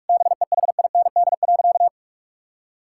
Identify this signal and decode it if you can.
Morse code, heard as a keyed tone, about 46 wpm, 700 Hz. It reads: BEHIND.
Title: Sentences from Top 1000 Words - Review Part 5 - 45wpm